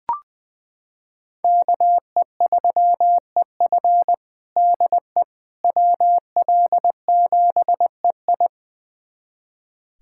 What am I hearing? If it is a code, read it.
Morse code, heard as a keyed tone, about 21 wpm, 700 Hz. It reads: KE3EF DE WL7EI
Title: USA Callsign DE Callsign - 5 Characters - 20wpm